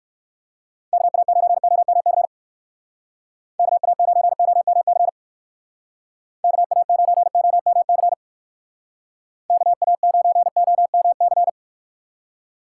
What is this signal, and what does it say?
Morse code, heard as a keyed tone, about 55 wpm, 700 Hz. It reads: KA9OMC KA9OMC KA9OMC KA9OMC
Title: Speed Racing - USA Call Signs 45wpm